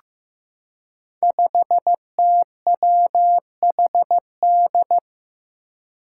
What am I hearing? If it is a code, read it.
Morse code, heard as a keyed tone, about 15 wpm, 700 Hz. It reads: 5TWHD
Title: Five Challenging Letter-Number Combinations 15wpm